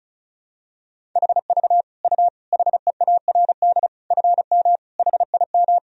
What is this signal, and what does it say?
Morse code, heard as a keyed tone, about 36 wpm, 700 Hz. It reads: HV U HEARD FM HIM
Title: Sentences from Top 400 Words and Abbreviations 35wpm